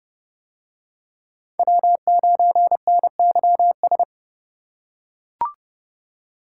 Morse code, heard as a keyed tone, about 29 wpm, 700 Hz. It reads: W9NYS